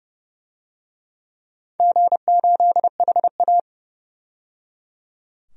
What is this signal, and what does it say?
Morse code, heard as a keyed tone, about 30 wpm, 700 Hz. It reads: G8HA
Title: DXCC-100 Call Signs - 4 Characters - 30wpm